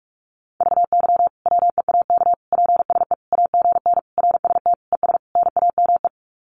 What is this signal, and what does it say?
Morse code, heard as a keyed tone, about 44 wpm, 700 Hz. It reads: VY WEAK PSE AGN RST ES NAME
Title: Common QSO Phrases 45wpm